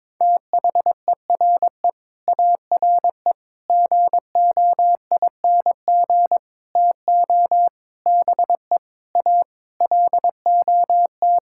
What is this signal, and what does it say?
Morse code, heard as a keyed tone, about 22 wpm, 700 Hz. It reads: THERE ARE GOING TO BE A LOT
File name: practice / 22wpm / Sets of 8 Words - 22wpm